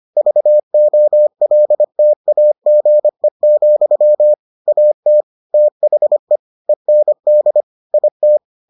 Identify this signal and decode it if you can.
Morse code, heard as a keyed tone, about 25 wpm, 600 Hz. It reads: VOLTAGE, AT THE END IT